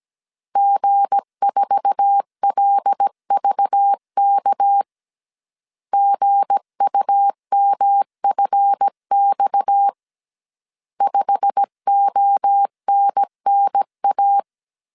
Morse code, heard as a keyed tone, about 17 wpm, 800 Hz. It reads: G4LVK GUMFX 5ONNA
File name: CW / DET / 1 9 1500C S A 16wpm 2828